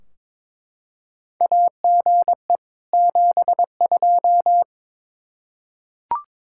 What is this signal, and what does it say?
Morse code, heard as a keyed tone, about 22 wpm, 700 Hz. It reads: AGE 72
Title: Practice QSO - Exchange 3 - More Info - 22wpm